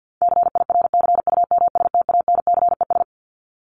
Morse code, heard as a keyed tone, about 50 wpm, 700 Hz. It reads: CIRCUMSTANCES
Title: ES:S Suffix - Top 100 Words with Base Words 50wpm